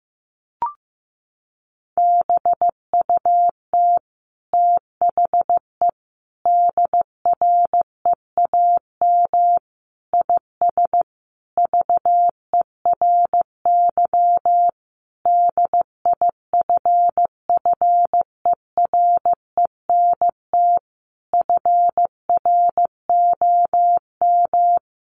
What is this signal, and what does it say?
Morse code, heard as a keyed tone, about 14 wpm, 700 Hz. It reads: BUT THE DREAM IS VERY DIFFERENT FROM
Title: Sentences from Top 1500 Words 15wpm